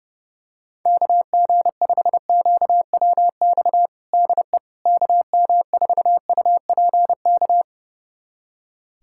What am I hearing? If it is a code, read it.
Morse code, heard as a keyed tone, about 30 wpm, 700 Hz. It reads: KG5QWX DE KM4UPK
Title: USA Callsign DE Callsign - 6 Characters - 30wpm